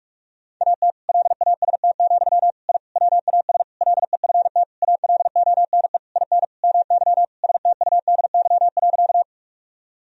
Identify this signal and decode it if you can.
Morse code, heard as a keyed tone, about 46 wpm, 700 Hz. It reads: AT LAST, I WAS LEFT ALONE IN MY STUDY.
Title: Sentences from Top 1000 Words - Review Part 1 - 45wpm